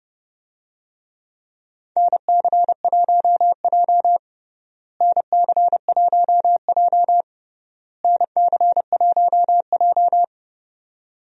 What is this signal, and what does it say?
Morse code, heard as a keyed tone, about 29 wpm, 700 Hz. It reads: NC1J NC1J NC1J
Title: USA Call Signs - 4 Characters - Repeated 3x - 30wpm